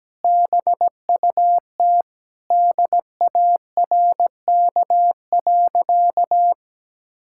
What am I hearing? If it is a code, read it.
Morse code, heard as a keyed tone, about 17 wpm, 700 Hz. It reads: BUT DARK.
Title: Sentences from Top 1600 Words 17wpm